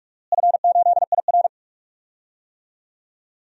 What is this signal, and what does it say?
Morse code, heard as a keyed tone, about 46 wpm, 700 Hz. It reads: F8IR
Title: DXCC-100 Call Signs - 4 Characters - Repeated 3x - 45wpm